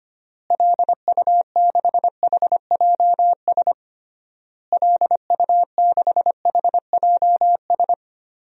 Morse code, heard as a keyed tone, about 25 wpm, 700 Hz. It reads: LU6HJS LU6HJS
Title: DXCC-100 Call Signs - 6 Characters - Repeated 3x - 25wpm